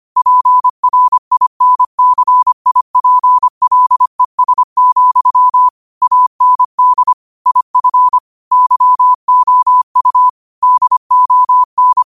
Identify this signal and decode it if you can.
Morse code, heard as a keyed tone, about 25 wpm, 1000 Hz. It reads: PRINCIPLES, AND IF YOU DON
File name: qotdfiles / 25wpm